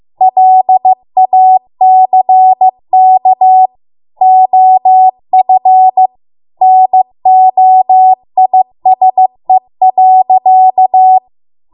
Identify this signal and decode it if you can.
Morse code, heard as a keyed tone, about 15 wpm, 750 Hz. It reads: LACK OF NOISE.